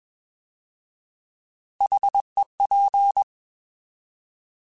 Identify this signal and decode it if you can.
Morse code, heard as a keyed tone, about 21 wpm, 800 Hz. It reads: HEP